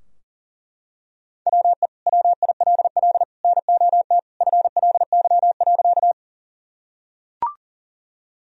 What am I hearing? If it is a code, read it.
Morse code, heard as a keyed tone, about 40 wpm, 700 Hz. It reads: WE WILL NOT FLY.
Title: Sentences from Top 1000 Words - Review Part 2 - 40wpm